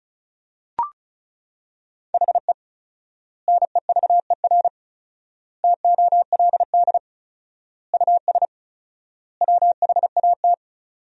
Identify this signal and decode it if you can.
Morse code, heard as a keyed tone, about 36 wpm, 700 Hz. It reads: HE NEVER TOLD US WHAT
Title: 4x Word-Spacing - Sentences from Top 400 Words - 35wpm